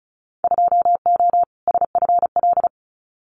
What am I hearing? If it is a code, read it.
Morse code, heard as a keyed tone, about 36 wpm, 700 Hz. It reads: 2O SFL
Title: Winter Field Day Exchanges 35wpm